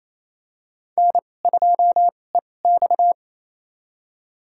Morse code, heard as a keyed tone, about 28 wpm, spaced as for 20 wpm, 700 Hz. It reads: N2EX
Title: USA Call Signs - 4 Characters - 2x Character Spacing - 28wpm